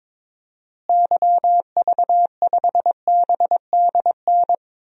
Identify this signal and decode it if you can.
Morse code, heard as a keyed tone, about 22 wpm, 700 Hz. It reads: YV5BDN
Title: DXCC-100 Call Signs - 6 Characters - Repeated 2x - 22wpm